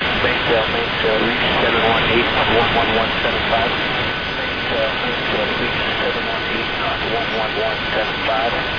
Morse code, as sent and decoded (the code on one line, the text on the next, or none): none